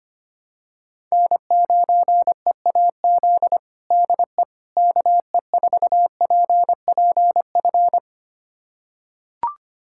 Morse code, heard as a keyed tone, about 25 wpm, 700 Hz. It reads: N9EAZ DE KE4PPF
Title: USA Callsign DE Callsign 25wpm